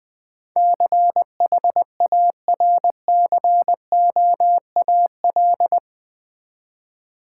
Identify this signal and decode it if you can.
Morse code, heard as a keyed tone, about 20 wpm, 700 Hz. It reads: CHARCOAL
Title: Easy Words with Hints - 20wpm